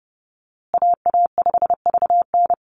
Morse code, heard as a keyed tone, about 30 wpm, 700 Hz. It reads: AA5VN
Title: USA Call Signs Repeated 3x 30wpm